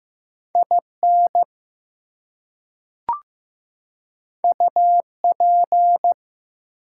Morse code, heard as a keyed tone, about 15 wpm, 700 Hz. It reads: IN UP
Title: Two Letter Words - 15wpm